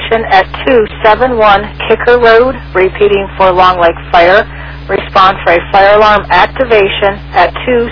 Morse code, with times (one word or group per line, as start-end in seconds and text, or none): none